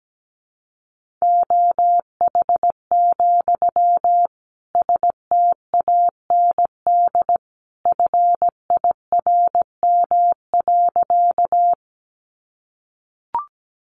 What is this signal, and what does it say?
Morse code, heard as a keyed tone, about 17 wpm, 700 Hz. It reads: OH, STAND FIRM.
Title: Sentences from Top 1400 Words 17wpm